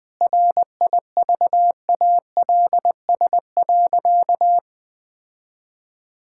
Morse code, heard as a keyed tone, about 20 wpm, 700 Hz. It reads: RIVALS.
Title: Sentences with 7-Word Phrases - 20wpm